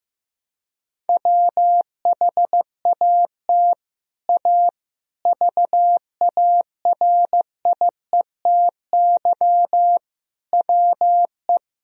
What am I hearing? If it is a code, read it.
Morse code, heard as a keyed tone, about 15 wpm, 700 Hz. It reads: WHAT A VARIETY WE
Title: Sentences from Top 1000 Words - Review Part 4 - 15wpm